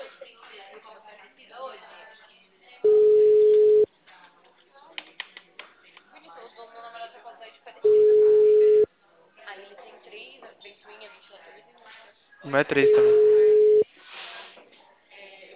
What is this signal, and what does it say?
Ringback tone (Europe, 425 Hz), with speech.